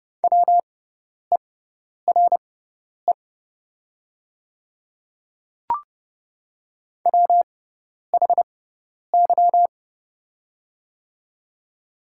Morse code, heard as a keyed tone, about 30 wpm, spaced as for 10 wpm, 700 Hz. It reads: WERE WHY
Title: 6x Character Spacing - Top 200 Words - 30wpm